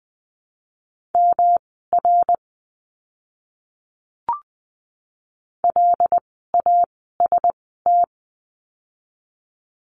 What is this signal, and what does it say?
Morse code, heard as a keyed tone, about 20 wpm, spaced as for 15 wpm, 700 Hz. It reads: MR LAST
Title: 244 Lesson 39f - Letter z - Words 20wpm